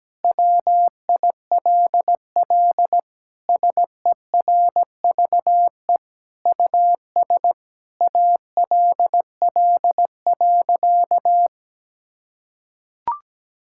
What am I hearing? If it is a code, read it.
Morse code, heard as a keyed tone, about 17 wpm, 700 Hz. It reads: WILL SERVE US ALL.
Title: Sentences from Top 1000 Words - Review Part 5 - 17wpm